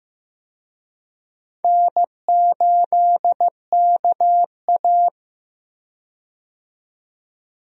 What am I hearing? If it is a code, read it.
Morse code, heard as a keyed tone, about 15 wpm, 700 Hz. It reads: N8KA
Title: USA Call Signs - 4 Characters - 15wpm